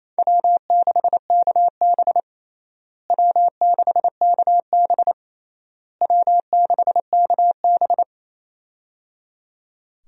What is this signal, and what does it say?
Morse code, heard as a keyed tone, about 28 wpm, 700 Hz. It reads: W6KB W6KB W6KB
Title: USA Call Signs - 4 Characters - Repeated 3x - 28wpm